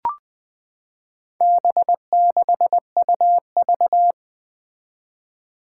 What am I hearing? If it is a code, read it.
Morse code, heard as a keyed tone, about 20 wpm, 700 Hz. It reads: B6UV
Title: Four Challenging Letter-Number Combinations 20wpm